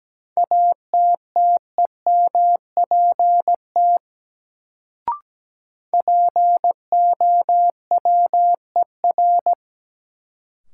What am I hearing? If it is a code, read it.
Morse code, heard as a keyed tone, about 17 wpm, 700 Hz. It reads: ATTEMPT POWER
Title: Sets of 4 Words - New Word Drill - 17wpm